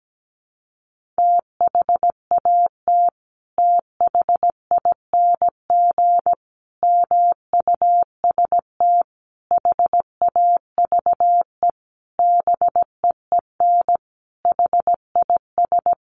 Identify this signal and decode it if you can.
Morse code, heard as a keyed tone, about 17 wpm, 700 Hz. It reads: THAT THING MUST HAVE BEEN HIS
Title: Sentences from Top 1500 Words 17wpm